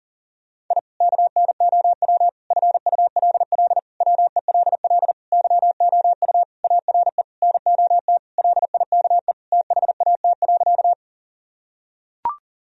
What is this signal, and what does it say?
Morse code, heard as a keyed tone, about 40 wpm, 700 Hz. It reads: I KNOW FULL WELL YOU ARE NOT LIKE THAT.
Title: Sentences from Top 1000 Words - Review Part 3 - 40wpm